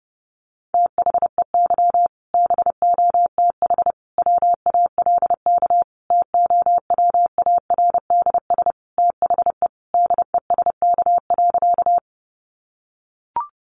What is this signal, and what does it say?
Morse code, heard as a keyed tone, about 30 wpm, 700 Hz. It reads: THEY BOTH WALK TOWARDS THE DESK.
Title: Additional Easy Phrases 30wpm